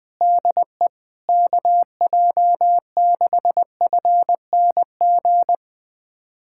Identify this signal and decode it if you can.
Morse code, heard as a keyed tone, about 20 wpm, 700 Hz. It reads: DE KJ6FNG
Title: USA Callsign DE Callsign - 6 Characters - 20wpm